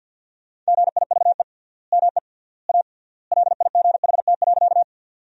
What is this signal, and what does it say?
Morse code, heard as a keyed tone, about 51 wpm, 700 Hz. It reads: GIVE ME A LIGHT.
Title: 3x Word-Spacing - Sentences from Top 200 Words - 50wpm